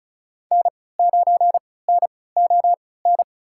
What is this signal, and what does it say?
Morse code, heard as a keyed tone, about 36 wpm, spaced as for 20 wpm, 700 Hz. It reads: N9NON
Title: USA Call Signs - 3x Character Spacing - 35wpm